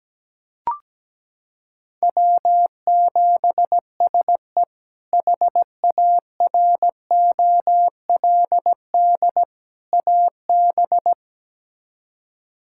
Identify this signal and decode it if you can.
Morse code, heard as a keyed tone, about 17 wpm, 700 Hz. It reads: W7SE HAROLD AB
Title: K1USN SST QSO Practice - 17wpm